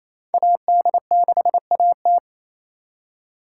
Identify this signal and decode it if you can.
Morse code, heard as a keyed tone, about 34 wpm, 700 Hz. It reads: AD6AT
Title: Speed Racing - USA 2x2 Call Signs 28wpm